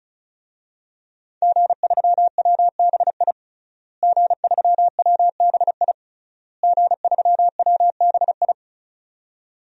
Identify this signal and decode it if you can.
Morse code, heard as a keyed tone, about 36 wpm, 700 Hz. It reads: G3WBI G3WBI G3WBI
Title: DXCC-100 Call Signs - 5 Characters - Repeated 3x - 35wpm